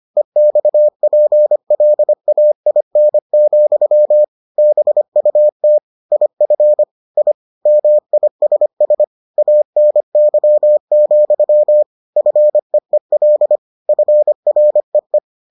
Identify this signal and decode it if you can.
Morse code, heard as a keyed tone, about 25 wpm, 600 Hz. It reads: EXPLAIN, BUT IF I MISS ANY, FEEL FREE